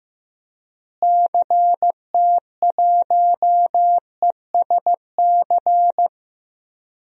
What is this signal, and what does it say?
Morse code, heard as a keyed tone, about 16 wpm, 700 Hz. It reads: CT1ESC
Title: DXCC-100 Call Signs - 6 Characters - Repeated 2x - 15wpm